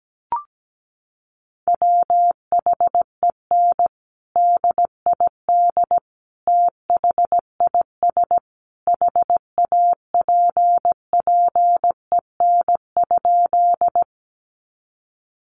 Morse code, heard as a keyed tone, about 17 wpm, 700 Hz. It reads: WHEN DID THIS HAPPEN?
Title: Sentences from Top 400 Words 17wpm